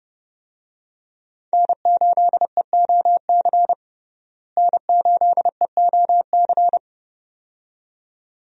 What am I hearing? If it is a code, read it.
Morse code, heard as a keyed tone, about 32 wpm, 700 Hz. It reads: N8EOC N8EOC
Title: USA Call Signs - 5 Characters - Repeated 2x - 30wpm